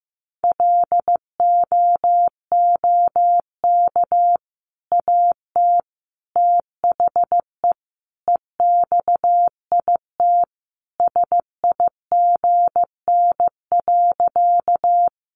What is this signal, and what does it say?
Morse code, heard as a keyed tone, about 15 wpm, 700 Hz. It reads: LOOK AT THE EXIT SIGN.
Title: Sentences from Top 1100 Words 15wpm